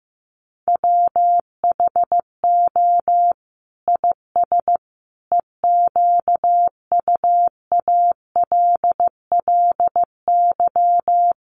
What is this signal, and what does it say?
Morse code, heard as a keyed tone, about 15 wpm, 700 Hz. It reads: WHO IS EQUALLY
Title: Sentences from Top 1400 Words 15wpm